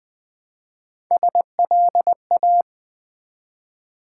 Morse code, heard as a keyed tone, about 20 wpm, 700 Hz. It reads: SLA
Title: Build A Word - 1 Letter At A Time - 3 to 5-Letter Words - 20wpm